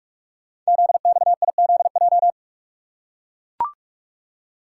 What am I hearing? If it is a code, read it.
Morse code, heard as a keyed tone, about 44 wpm, 700 Hz. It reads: ZXIZJ
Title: Code Groups - 5 Letters - 45wpm